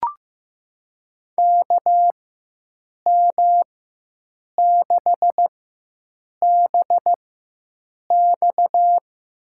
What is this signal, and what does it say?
Morse code, heard as a keyed tone, about 15 wpm, spaced as for 7 wpm, 700 Hz. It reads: KM6BX
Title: USA Call Signs - 4x Character Spacing - 15wpm